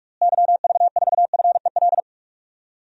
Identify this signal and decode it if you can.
Morse code, heard as a keyed tone, about 46 wpm, 700 Hz. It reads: YV4FEL
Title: DXCC-100 Call Signs - 6 Characters - Repeated 3x - 45wpm